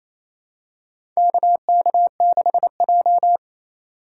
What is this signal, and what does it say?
Morse code, heard as a keyed tone, about 28 wpm, 700 Hz. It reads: KK6J
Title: USA Call Signs - 4 Characters - 2x Character Spacing - 28wpm